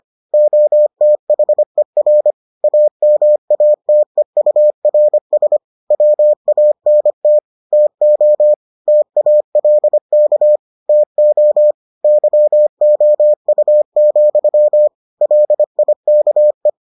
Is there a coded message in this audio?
Yes: Morse code from 0.3 to 16.7 s, heard as a keyed tone, about 25 wpm, 600 Hz. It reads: OTHER AMATEURS WANT TO TALK TO YOU, LIKE